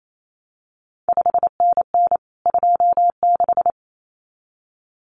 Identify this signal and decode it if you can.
Morse code, heard as a keyed tone, about 28 wpm, 700 Hz. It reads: 5NN 26
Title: CQ WW DX Contest Exchange Encore - 28wpm